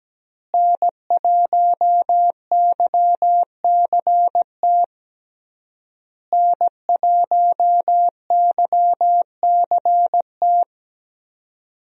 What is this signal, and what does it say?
Morse code, heard as a keyed tone, about 17 wpm, 700 Hz. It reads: N1YCT N1YCT
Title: USA Call Signs - 5 Characters - Repeated 3x - 17wpm